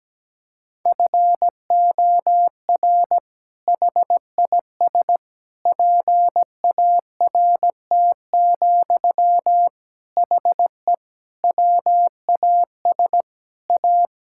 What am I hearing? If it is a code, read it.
Morse code, heard as a keyed tone, about 17 wpm, 700 Hz. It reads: FOR HIS PART, HE WAS A
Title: Sentences from Top 1000 Words - Review Part 5 - 17wpm